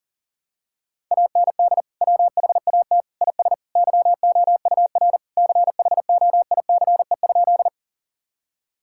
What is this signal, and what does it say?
Morse code, heard as a keyed tone, about 40 wpm, 700 Hz. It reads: AND WHAT IS YOUR CHOICE?
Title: Sentences from Top 1000 Words - Review Part 6 - 40wpm